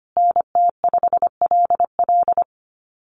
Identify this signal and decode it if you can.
Morse code, heard as a keyed tone, about 25 wpm, 700 Hz. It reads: NT5LL
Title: USA Call Signs Repeated 3x 25wpm